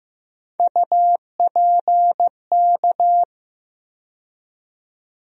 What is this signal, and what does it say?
Morse code, heard as a keyed tone, about 15 wpm, 700 Hz. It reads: UPK
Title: USA Call Sign Postfixes - 3 Letters 15wpm